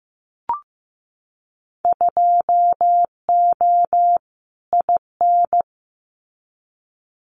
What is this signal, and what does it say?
Morse code, heard as a keyed tone, about 15 wpm, 700 Hz. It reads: 2O IN
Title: Winter Field Day Exchanges 15wpm